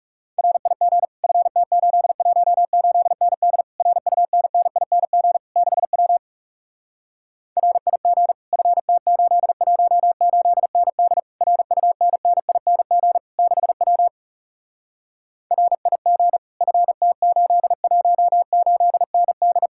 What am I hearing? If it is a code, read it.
Morse code, heard as a keyed tone, about 40 wpm, 700 Hz. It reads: RIG FT818ND RUNNING 6W RIG FT818ND RUNNING 6W RIG FT818ND